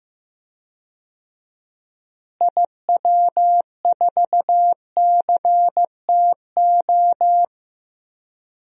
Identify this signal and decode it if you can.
Morse code, heard as a keyed tone, about 15 wpm, 700 Hz. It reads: IW4CTO
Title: DXCC-100 Call Signs - 6 Characters - Repeated 3x - 15wpm